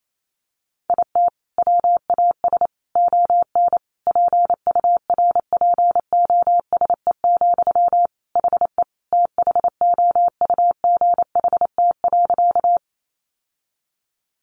Morse code, heard as a keyed tone, about 28 wpm, 700 Hz. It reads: IT WAS ON PURPOSE, HE THOUGHT.
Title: Sentences from Top 800 Words 28wpm